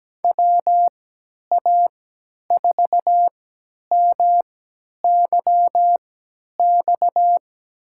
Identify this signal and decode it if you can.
Morse code, heard as a keyed tone, about 17 wpm, spaced as for 10 wpm, 700 Hz. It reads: WA4MYX
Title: USA Call Signs - 6 Characters - 3x Character Spacing - 17wpm